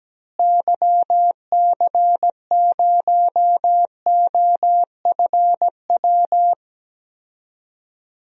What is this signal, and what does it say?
Morse code, heard as a keyed tone, about 17 wpm, 700 Hz. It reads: YC0OFW
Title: DXCC-100 Call Signs - 6 Characters - Repeated 3x - 17wpm